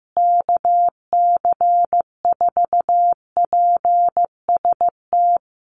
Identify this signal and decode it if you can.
Morse code, heard as a keyed tone, about 15 wpm, 700 Hz. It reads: KC4PST
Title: USA Call Signs Repeated 3x 15wpm